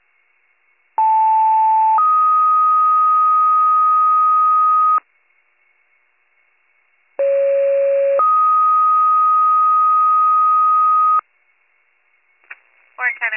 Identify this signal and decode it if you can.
Two-tone page, heard as a one-second tone, then a three-second tone, about 880 then 1250 Hz, about 550 then 1250 Hz, with speech. It is heard from 1.0 to 11.2 s.